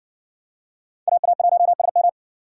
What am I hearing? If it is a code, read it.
Morse code, heard as a keyed tone, about 62 wpm, 700 Hz. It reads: AA1SM